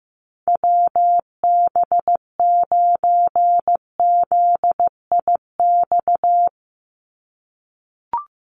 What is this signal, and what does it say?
Morse code, heard as a keyed tone, about 15 wpm, 700 Hz. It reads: WB9ZIX
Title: USA Call Signs 15wpm